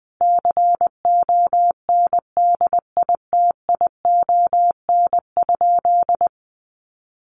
Morse code, heard as a keyed tone, about 20 wpm, 700 Hz. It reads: CONDITION?
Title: Sentences from Top 1300 Words 20wpm